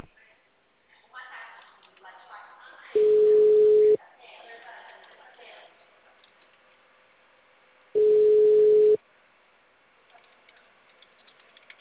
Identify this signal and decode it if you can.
Ringback tone (Europe, 425 Hz), with speech.